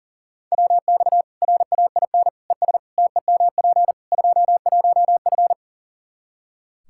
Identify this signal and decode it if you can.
Morse code, heard as a keyed tone, about 40 wpm, 700 Hz. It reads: WX RAIN ES TEMP 21F
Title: Practice QSO - Exchange 2 - Info - 40wpm